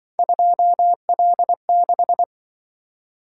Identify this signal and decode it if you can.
Morse code, heard as a keyed tone, about 24 wpm, 700 Hz. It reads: 2L6
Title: Speed-Racing - Three Letter-Number Combinations - 20wpm